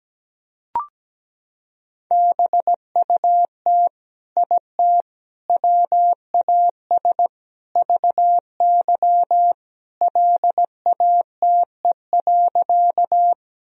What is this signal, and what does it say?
Morse code, heard as a keyed tone, about 17 wpm, 700 Hz. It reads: BUT IT WAS VY LATE.
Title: Sentences from Top 300 Words and Abbreviations 17wpm